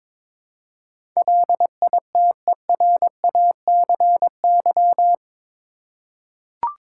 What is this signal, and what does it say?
Morse code, heard as a keyed tone, about 21 wpm, 700 Hz. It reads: LITERACY
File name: practice / 22wpm / Eight Letter Words - 22wpm